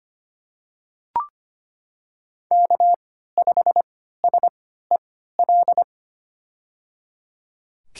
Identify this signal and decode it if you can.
Morse code, heard as a keyed tone, about 25 wpm, spaced as for 14 wpm, 700 Hz. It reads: K5SEL
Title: USA Call Signs - 5 Characters - 3x Character Spacing - 25wpm